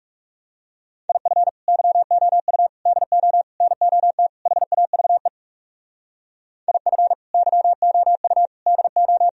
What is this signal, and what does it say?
Morse code, heard as a keyed tone, about 37 wpm, 700 Hz. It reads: IF YOU DO NOT HAVE IF YOU DO
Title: Speed-Racing - Sets of 5 Words 30wpm